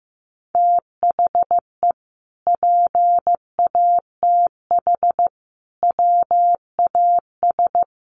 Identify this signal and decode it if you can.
Morse code, heard as a keyed tone, about 15 wpm, 700 Hz. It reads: THE PATH WAS
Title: Sentences from Top 1400 Words 15wpm